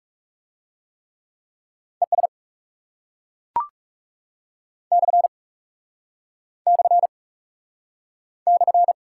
Morse code, heard as a keyed tone, about 40 wpm, 700 Hz. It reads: ES / / /